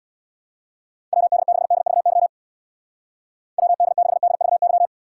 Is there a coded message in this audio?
Yes: Morse code from 1.1 to 4.9 s, heard as a keyed tone, about 63 wpm, 700 Hz. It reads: WD6DVY WD6DVY